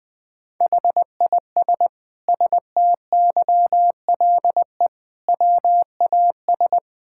Morse code, heard as a keyed tone, about 20 wpm, 700 Hz. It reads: HIS STYLE WAS
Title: Sentences from Top 1000 Words - Review Part 5 - 20wpm